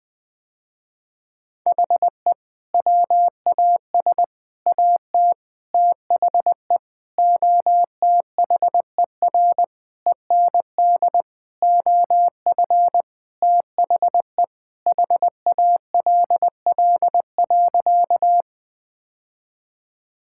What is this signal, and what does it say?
Morse code, heard as a keyed tone, about 20 wpm, 700 Hz. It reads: HE WAS AT THE OTHER END OF THE HALL.
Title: Sentences with 5-Word Phrases - Spoken Beginning - 20wpm